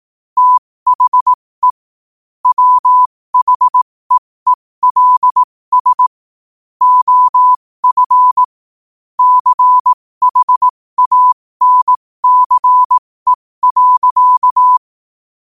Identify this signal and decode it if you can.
Morse code, heard as a keyed tone, about 18 wpm, spaced as for 15 wpm, 1000 Hz. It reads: THE WHEELS OF CHANCE.
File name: qotdfiles / 15wpm